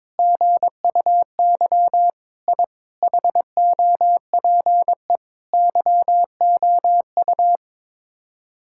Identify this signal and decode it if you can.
Morse code, heard as a keyed tone, about 22 wpm, 700 Hz. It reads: GUY I HOPE YOU
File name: practice / 22wpm / Sets of 8 Words - 22wpm